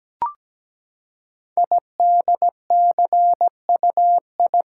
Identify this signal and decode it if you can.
Morse code, heard as a keyed tone, about 17 wpm, 700 Hz. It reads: IDCUI
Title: Code Groups - 5 Letter:Numbers - 17wpm